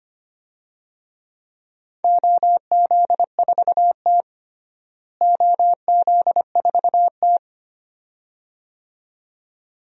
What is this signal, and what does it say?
Morse code, heard as a keyed tone, about 25 wpm, 700 Hz. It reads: OZ4T OZ4T
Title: DXCC-100 Call Signs - 4 Characters - Repeated 2x - 25wpm